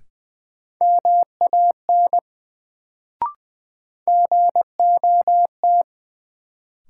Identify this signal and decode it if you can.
Morse code, heard as a keyed tone, about 20 wpm, 700 Hz. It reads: MAN GOT
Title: Sets of 2 Words - New Word Drill - 20wpm